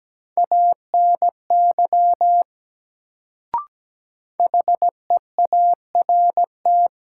Morse code, heard as a keyed tone, about 17 wpm, 700 Hz. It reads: ANY HEART
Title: Sets of 3 Words - New Word Drill - 17wpm